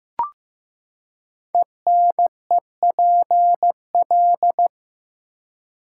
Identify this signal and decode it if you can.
Morse code, heard as a keyed tone, about 15 wpm, 700 Hz. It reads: ENEPL
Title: Code Groups - 5 Letters - 15wpm